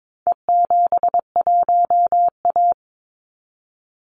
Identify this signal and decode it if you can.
Morse code, heard as a keyed tone, about 22 wpm, 700 Hz. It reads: E71A
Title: Call Signs Repeated 3x - 22wpm